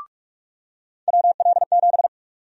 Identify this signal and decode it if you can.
Morse code, heard as a keyed tone, about 47 wpm, 700 Hz. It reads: WL7
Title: USA Call Sign Prefixes - 45wpm